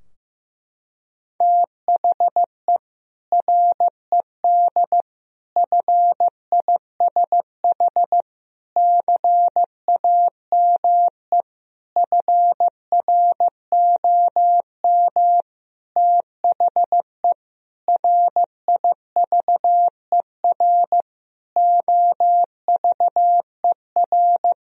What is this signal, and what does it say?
Morse code, heard as a keyed tone, about 15 wpm, 700 Hz. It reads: THE RED FISH CAME FROM THE RIVER OVER
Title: Sentences from Top 1000 Words - Review Part 2 - 15wpm